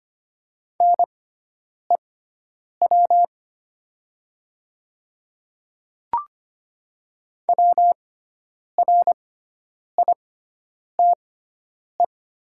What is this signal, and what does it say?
Morse code, heard as a keyed tone, about 25 wpm, spaced as for 9 wpm, 700 Hz. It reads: NEW WRITE
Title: 6x Character Spacing - Top 300 Words - 25wpm